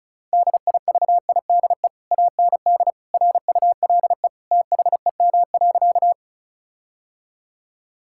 Morse code, heard as a keyed tone, about 34 wpm, 700 Hz. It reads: DIVIDE AND RULE THEM.
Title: Sentences from Top 1600 Words 35wpm